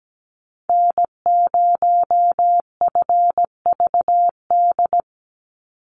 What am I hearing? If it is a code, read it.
Morse code, heard as a keyed tone, about 17 wpm, 700 Hz. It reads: N0FVD
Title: USA Call Signs 17wpm